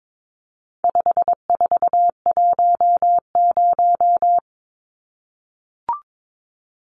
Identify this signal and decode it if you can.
Morse code, heard as a keyed tone, about 22 wpm, 700 Hz. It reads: 5410
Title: Numbers - 4-Digit 22wpm